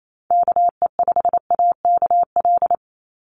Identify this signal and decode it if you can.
Morse code, heard as a keyed tone, about 28 wpm, 700 Hz. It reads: KE5AKL
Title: NA SOTA Chaser Callsigns 28wpm